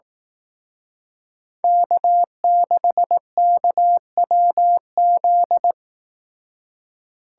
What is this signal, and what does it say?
Morse code, heard as a keyed tone, about 18 wpm, 700 Hz. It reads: K6KWZ
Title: Speed Racing - USA 1x3 Call Signs 15wpm